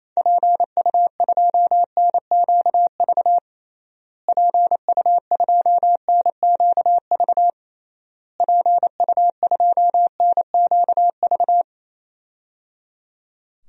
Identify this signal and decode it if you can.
Morse code, heard as a keyed tone, about 28 wpm, 700 Hz. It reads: PU2NQV PU2NQV PU2NQV
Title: DXCC-100 Call Signs - 6 Characters - Repeated 3x - 28wpm